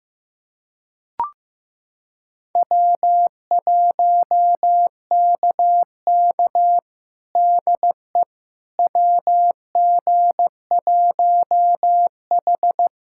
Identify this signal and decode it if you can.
Morse code, heard as a keyed tone, about 15 wpm, 700 Hz. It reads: W1KK DE WG1H
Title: USA Callsign DE Callsign - 4 Characters - 15wpm